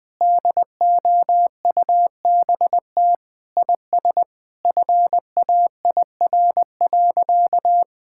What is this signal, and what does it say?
Morse code, heard as a keyed tone, about 20 wpm, 700 Hz. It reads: DOUBT IS FAIR.
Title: Sentences with 8-Word Phrases - Spoken Beginning - 20wpm